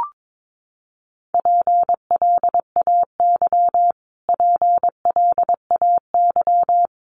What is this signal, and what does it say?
Morse code, heard as a keyed tone, about 22 wpm, 700 Hz. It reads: PLAY PLAY
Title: Top 500 Words Repeated 22wpm